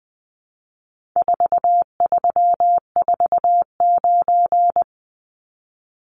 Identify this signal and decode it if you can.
Morse code, heard as a keyed tone, about 20 wpm, 700 Hz. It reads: 4349
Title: Numbers - 4-Digit 20wpm